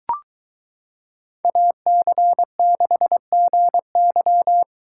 Morse code, heard as a keyed tone, about 22 wpm, 700 Hz. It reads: AC6GY